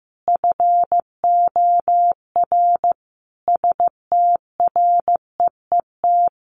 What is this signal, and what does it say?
Morse code, heard as a keyed tone, about 15 wpm, 700 Hz. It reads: FOR STREET
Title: Sentences from Top 1500 Words 15wpm